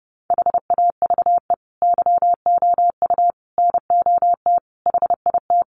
Morse code, heard as a keyed tone, about 30 wpm, 700 Hz. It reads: HAVE YOU NOT HIT
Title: Sentences from Top 1200 Words 30wpm